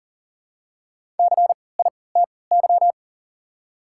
Morse code, heard as a keyed tone, about 40 wpm, spaced as for 23 wpm, 700 Hz. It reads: CITY
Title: 3x Character Spacing - Top 300 Words - 40wpm